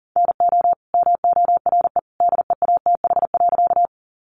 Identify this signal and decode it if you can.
Morse code, heard as a keyed tone, about 40 wpm, 700 Hz. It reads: NO MORE DEATH.
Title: Sentences from Top 1100 Words 40wpm